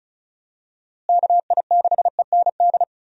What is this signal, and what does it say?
Morse code, heard as a keyed tone, about 36 wpm, 700 Hz. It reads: KI6END
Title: USA Call Signs - 6 Characters - Repeated 3x - 35wpm